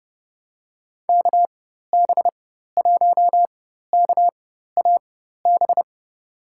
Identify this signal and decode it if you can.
Morse code, heard as a keyed tone, about 31 wpm, spaced as for 14 wpm, 700 Hz. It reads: KB1KAB
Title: USA Call Signs - 6 Characters - 4x Character Spacing - 30wpm